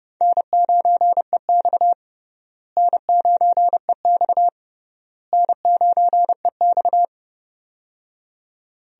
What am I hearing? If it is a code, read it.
Morse code, heard as a keyed tone, about 30 wpm, 700 Hz. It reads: N9EX N9EX N9EX
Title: USA Call Signs - 4 Characters - Repeated 3x - 30wpm